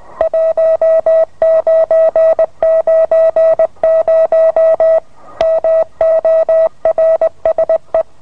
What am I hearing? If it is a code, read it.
Morse code, heard as a keyed tone, about 20 wpm, 650 Hz. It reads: 1990 MORSE